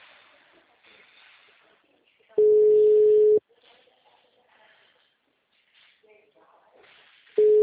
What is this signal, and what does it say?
Ringback tone (Europe, 425 Hz).
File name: out- -2013- - - .82631